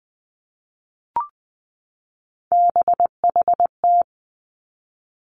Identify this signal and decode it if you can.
Morse code, heard as a keyed tone, about 20 wpm, 700 Hz. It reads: BHT
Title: Three Challenging Letter-Number Combinations 20wpm